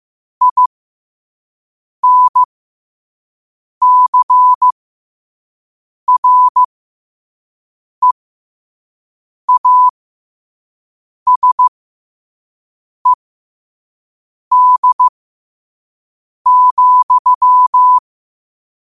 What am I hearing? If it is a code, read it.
Morse code, heard as a keyed tone, about 15 wpm, spaced as for 5 wpm, 1000 Hz. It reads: INCREASED,
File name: qotdfiles / 5wpm